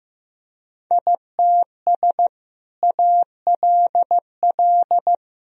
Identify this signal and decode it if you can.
Morse code, heard as a keyed tone, about 15 wpm, 700 Hz. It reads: ITS ALL
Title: Sentences from Top 1000 Words - Review Part 2 - 15wpm